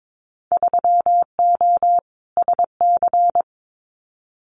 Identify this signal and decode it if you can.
Morse code, heard as a keyed tone, about 22 wpm, 700 Hz. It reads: 3O SC
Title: Winter Field Day Exchanges 22wpm